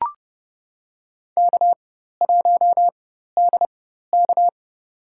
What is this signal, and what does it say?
Morse code, heard as a keyed tone, about 30 wpm, spaced as for 14 wpm, 700 Hz. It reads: K1DK
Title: USA Call Signs - 4 Characters - 4x Character Spacing - 30wpm